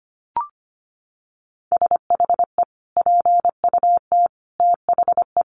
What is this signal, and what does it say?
Morse code, heard as a keyed tone, about 25 wpm, 700 Hz. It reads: SHE PUT THE
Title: Sentences from Top 1400 Words 25wpm